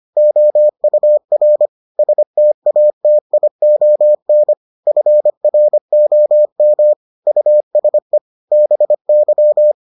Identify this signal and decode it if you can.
Morse code, heard as a keyed tone, about 25 wpm, 600 Hz. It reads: OUR STATION FROM USE BY